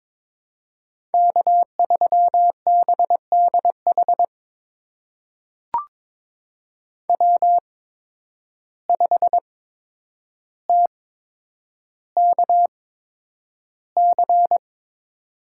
Morse code, heard as a keyed tone, about 22 wpm, spaced as for 9 wpm, 700 Hz. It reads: K3BDH W5TKC
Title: USA Call Signs - 5 Characters - 8x Character Spacing - 22wpm